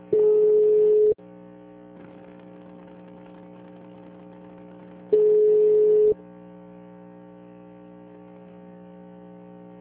Ringback tone (Europe, 425 Hz).
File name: out- -2003- - - .50267